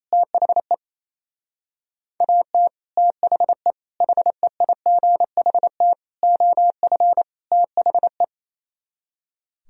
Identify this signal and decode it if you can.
Morse code, heard as a keyed tone, about 34 wpm, 700 Hz. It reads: THE AT THE HEIGHT OF THE
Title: Speed-Racing - Sets of 5 Words 28wpm